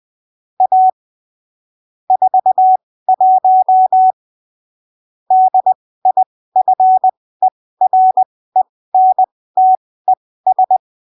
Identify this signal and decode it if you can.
Morse code, heard as a keyed tone, about 20 wpm, spaced as for 15 wpm, 750 Hz. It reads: A 41 DIFERENTES
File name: arquivos / texto / texto 12 20ppm